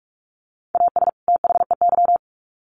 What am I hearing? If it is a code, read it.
Morse code, heard as a keyed tone, about 44 wpm, 700 Hz. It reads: AS THEY
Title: Sets of 2 Words 45wpm